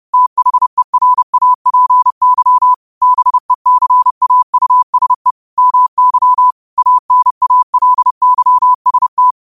Morse code, heard as a keyed tone, about 30 wpm, 1000 Hz. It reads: THERAPY BECAUSE MY ANALYST